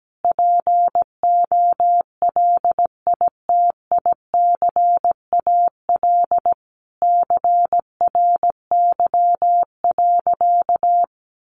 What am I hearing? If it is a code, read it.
Morse code, heard as a keyed tone, about 17 wpm, 700 Hz. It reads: POLITICAL CRY.
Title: Sentences from Top 1400 Words 17wpm